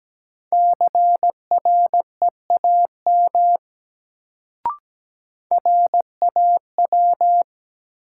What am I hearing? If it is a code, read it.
Morse code, heard as a keyed tone, about 17 wpm, 700 Hz. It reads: CREAM RAW
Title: Sentences from Top 1500 Words - New Word Drill - 17wpm